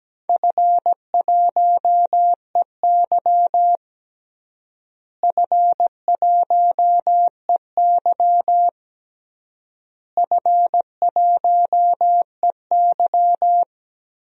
Morse code, heard as a keyed tone, about 17 wpm, 700 Hz. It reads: F1EY F1EY F1EY
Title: DXCC-100 Call Signs - 4 Characters - Repeated 3x - 17wpm